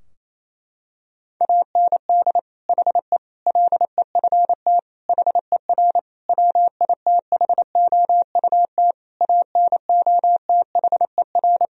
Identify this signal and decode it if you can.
Morse code, heard as a keyed tone, about 28 wpm, 700 Hz. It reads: AND HE LEFT HER WITHOUT ANOTHER
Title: Sentences from Top 1000 Words - Review Part 3 - 28wpm